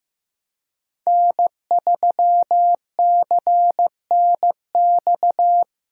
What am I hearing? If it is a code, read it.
Morse code, heard as a keyed tone, about 15 wpm, 700 Hz. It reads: N3CNX
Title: USA Call Signs - 5 Characters - Repeated 2x - 15wpm